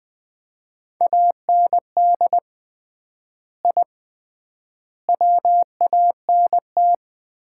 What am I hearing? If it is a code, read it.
Morse code, heard as a keyed tone, about 20 wpm, 700 Hz. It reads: AND I WANT
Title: 3x Word-Spacing - Sets of 4 Words - 20wpm